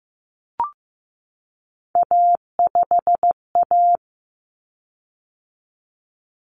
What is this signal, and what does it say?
Morse code, heard as a keyed tone, about 15 wpm, 700 Hz. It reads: A5A
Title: Common Cut Numbers - 3-Digit 15wpm